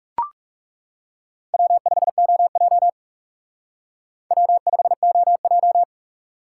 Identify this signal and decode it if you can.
Morse code, heard as a keyed tone, about 37 wpm, 700 Hz. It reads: W5OJ W5OJ
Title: Speed Racing - USA 1x2 Call Signs 30wpm